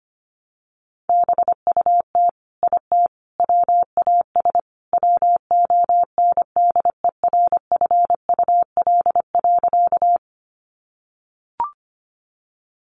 Morse code, heard as a keyed tone, about 25 wpm, 700 Hz. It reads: BUT IT WAS WONDERFUL.
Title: Sentences from Top 1100 Words 25wpm